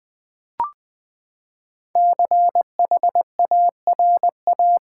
Morse code, heard as a keyed tone, about 20 wpm, 700 Hz. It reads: CHARA
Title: Build A Word - 5 Letters At A Time - 12-Letter Words - 20wpm